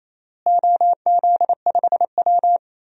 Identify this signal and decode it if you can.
Morse code, heard as a keyed tone, about 28 wpm, 700 Hz. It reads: OZ5W
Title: DXCC-100 Call Signs - 4 Characters - Repeated 2x - 28wpm